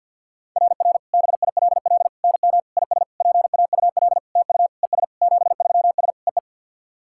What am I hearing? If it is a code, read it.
Morse code, heard as a keyed tone, about 49 wpm, 700 Hz. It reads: RR BILL NM IS PAUL TU ES 73S EE
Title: SOTA QSO Practice 50wpm